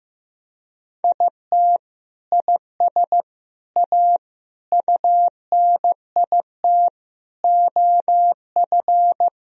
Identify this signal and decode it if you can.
Morse code, heard as a keyed tone, about 15 wpm, 700 Hz. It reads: IT IS A UNIT OF
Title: Sentences from Top 1000 Words - Review Part 2 - 15wpm